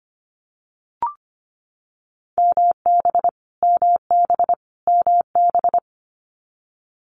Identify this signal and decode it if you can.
Morse code, heard as a keyed tone, about 25 wpm, 700 Hz. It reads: MB MB MB
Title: Canadian Provinces and Territories - Repeated 3x - 25wpm